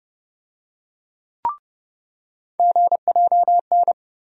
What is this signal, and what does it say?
Morse code, heard as a keyed tone, about 30 wpm, 700 Hz. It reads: GJN
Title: USA Call Sign Postfixes - 3 Letters 30wpm